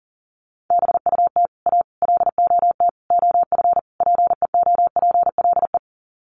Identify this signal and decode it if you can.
Morse code, heard as a keyed tone, about 40 wpm, 700 Hz. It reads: BUT A LOT OF PEOPLE
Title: Sets of 5 Words Encore2 40wpm